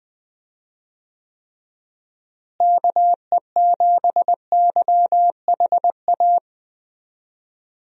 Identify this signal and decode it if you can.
Morse code, heard as a keyed tone, about 20 wpm, 700 Hz. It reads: KE7YHA
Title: USA Call Signs - 6 Characters - Repeated 3x - 20wpm